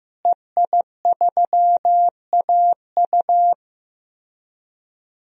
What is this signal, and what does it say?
Morse code, heard as a keyed tone, about 15 wpm, 700 Hz. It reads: EI3AU
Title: DXCC-100 Call Signs - 5 Characters - Repeated 2x - 15wpm